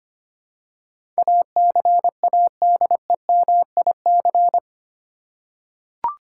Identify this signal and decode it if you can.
Morse code, heard as a keyed tone, about 25 wpm, 700 Hz. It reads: ACADEMIC
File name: 8 Letter Words - First 3 Letters Given - 25wpm